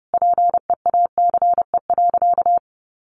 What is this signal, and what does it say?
Morse code, heard as a keyed tone, about 28 wpm, 700 Hz. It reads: PEACE.
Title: Sentences from Top 1500 Words 30wpm